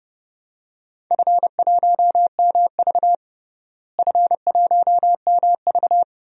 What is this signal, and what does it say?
Morse code, heard as a keyed tone, about 30 wpm, 700 Hz. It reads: F1MV F1MV
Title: DXCC-100 Call Signs - 4 Characters - Repeated 2x - 30wpm